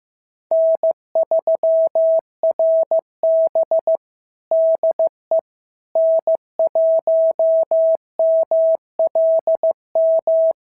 Morse code, heard as a keyed tone, about 14 wpm, 650 Hz. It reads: N3RB DE N1MLM
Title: SOTA QSO Practice 15wpm